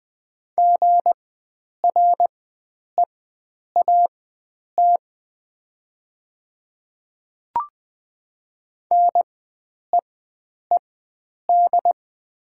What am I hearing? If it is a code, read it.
Morse code, heard as a keyed tone, about 20 wpm, spaced as for 9 wpm, 700 Hz. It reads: GREAT NEED
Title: 4x Character Spacing - Top 300 Words - 20wpm